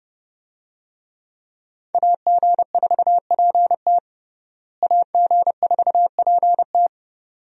Morse code, heard as a keyed tone, about 30 wpm, 700 Hz. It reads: AG4PT AG4PT
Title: USA Call Signs - 5 Characters - Repeated 2x - 30wpm